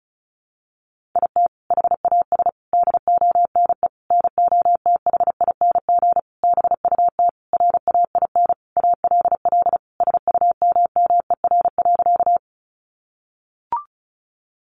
Morse code, heard as a keyed tone, about 34 wpm, 700 Hz. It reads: IT HAS DONE NOTHING BUT RAIN ALL SUMMER.
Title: Sentences from Top 1100 Words 35wpm